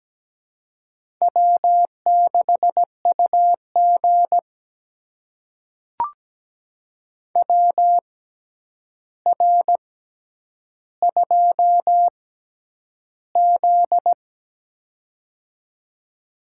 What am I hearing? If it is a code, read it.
Morse code, heard as a keyed tone, about 17 wpm, spaced as for 9 wpm, 700 Hz. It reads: W6UG WR2Z
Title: USA Call Signs - 4 Characters - 6x Character Spacing - 17wpm